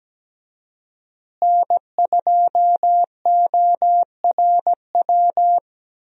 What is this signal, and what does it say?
Morse code, heard as a keyed tone, about 17 wpm, 700 Hz. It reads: N2ORW
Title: USA Call Signs - 5 Characters - Repeated 3x - 17wpm